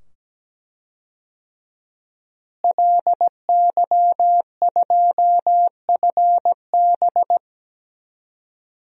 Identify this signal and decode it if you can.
Morse code, heard as a keyed tone, about 17 wpm, 700 Hz. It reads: LY2FB
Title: DXCC-100 Call Signs - 5 Characters - Repeated 2x - 17wpm